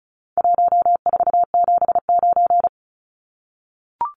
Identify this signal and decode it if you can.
Morse code, heard as a keyed tone, about 34 wpm, 700 Hz. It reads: 1479
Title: Numbers - Contest Style - 4-Digit 35wpm